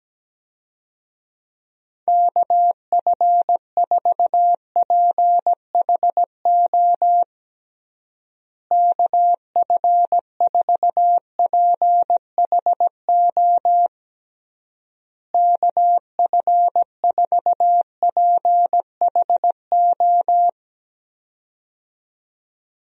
Morse code, heard as a keyed tone, about 17 wpm, 700 Hz. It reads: KF4PHO KF4PHO KF4PHO
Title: USA Call Signs - 6 Characters - Repeated 3x - 17wpm